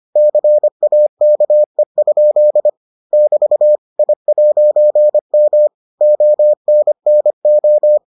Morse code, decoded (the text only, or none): CAKE? <BT> I'M ONNO